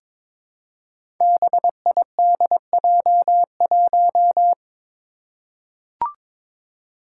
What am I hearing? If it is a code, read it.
Morse code, heard as a keyed tone, about 22 wpm, 700 Hz. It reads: BIDJ1
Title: Five Challenging Letter-Number Combinations 22wpm